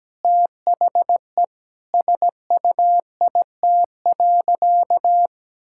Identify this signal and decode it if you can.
Morse code, heard as a keyed tone, about 17 wpm, 700 Hz. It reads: THE SUIT.
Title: Sentences from Top 1600 Words 17wpm